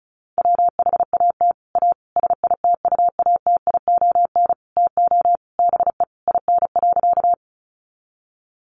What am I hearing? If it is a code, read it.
Morse code, heard as a keyed tone, about 34 wpm, 700 Hz. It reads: WHAT A SITUATION TO BE IN.
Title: Sentences from Top 1200 Words 35wpm